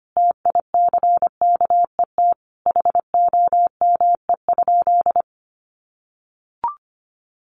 Morse code, heard as a keyed tone, about 25 wpm, 700 Hz. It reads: TICKET HOME?
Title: Sentences from Top 1400 Words 25wpm